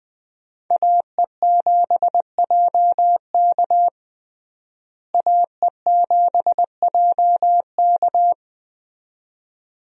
Morse code, heard as a keyed tone, about 20 wpm, 700 Hz. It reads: AE7JK AE7JK
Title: USA Call Signs - 5 Characters - Repeated 2x - 20wpm